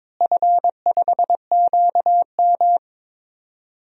Morse code, heard as a keyed tone, about 22 wpm, 700 Hz. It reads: F5QM
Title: DXCC-100 Call Signs - 4 Characters - Repeated 3x - 22wpm